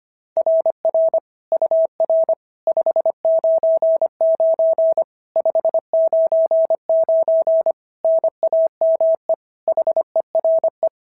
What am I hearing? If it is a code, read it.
Morse code, heard as a keyed tone, about 22 wpm, 650 Hz. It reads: RR UR 599 599 NAME HERE
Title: SOTA QSO Practice 25wpm